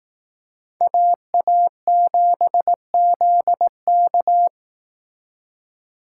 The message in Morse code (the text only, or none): AA7ZK